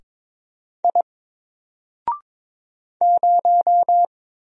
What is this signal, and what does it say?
Morse code, heard as a keyed tone, about 22 wpm, 700 Hz. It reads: I 0
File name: Single-Letter-Number-22wpm